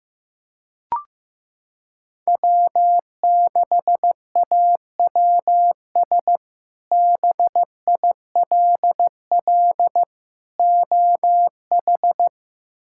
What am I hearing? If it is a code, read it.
Morse code, heard as a keyed tone, about 15 wpm, 700 Hz. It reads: W6AWS BILL OH
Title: K1USN SST QSO Practice - 15wpm